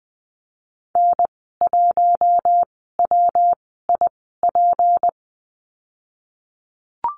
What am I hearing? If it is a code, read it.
Morse code, heard as a keyed tone, about 19 wpm, spaced as for 14 wpm, 700 Hz. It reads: N1WIP
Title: 154 Lesson 25g - Letter p - Call Signs 20wpm